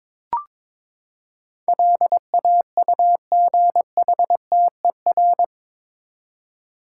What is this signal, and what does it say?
Morse code, heard as a keyed tone, about 22 wpm, 700 Hz. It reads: LAUGHTER
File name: practice / 22wpm / Eight Letter Words - 22wpm